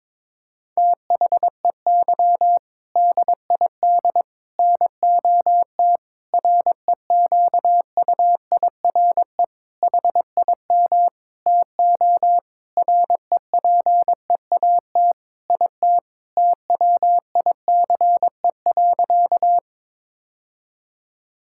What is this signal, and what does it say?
Morse code, heard as a keyed tone, about 22 wpm, 700 Hz. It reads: THEY DID NOT REQUIRE HIM TO REPEAT IT TWICE.
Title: Sentences from Top 1600 Words 22wpm